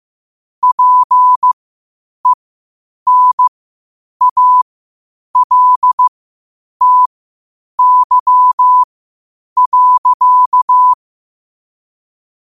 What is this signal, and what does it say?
Morse code, heard as a keyed tone, about 15 wpm, spaced as for 8 wpm, 1000 Hz. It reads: PENALTY.
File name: qotdfiles / 8wpm